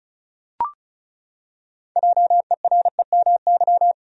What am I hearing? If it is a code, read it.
Morse code, heard as a keyed tone, about 34 wpm, 700 Hz. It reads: JEREMY
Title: USA Common Names - 35wpm